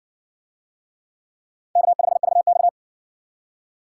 Morse code, heard as a keyed tone, about 53 wpm, 700 Hz. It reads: K5VX